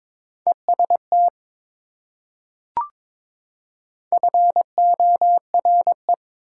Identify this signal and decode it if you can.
Morse code, heard as a keyed tone, about 21 wpm, 700 Hz. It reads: EST FORE
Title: Prefixes and Suffixes 22wpm